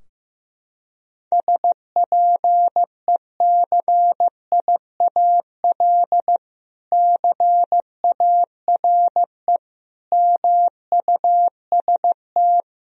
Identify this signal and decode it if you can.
Morse code, heard as a keyed tone, about 15 wpm, 700 Hz. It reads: SPECIAL CARE MUST
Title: Sentences from Top 1000 Words - Review Part 4 - 15wpm